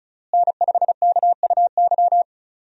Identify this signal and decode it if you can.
Morse code, heard as a keyed tone, about 34 wpm, 700 Hz. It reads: N5KUY
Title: USA Call Signs - 5 Characters - Repeated 3x - 35wpm